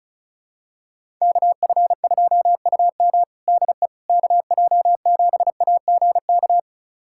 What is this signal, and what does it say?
Morse code, heard as a keyed tone, about 36 wpm, 700 Hz. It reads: KF2UM DE KJ7AGK
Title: USA Callsign DE Callsign 35wpm